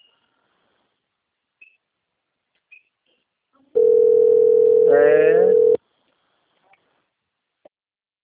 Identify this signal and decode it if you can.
Ringback tone (North America), 440 + 480 Hz.
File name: out- - - - - .4415